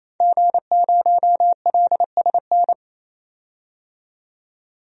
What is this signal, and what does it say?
Morse code, heard as a keyed tone, about 28 wpm, 700 Hz. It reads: G0LSN
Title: DXCC-100 Call Signs - 5 Characters - 28wpm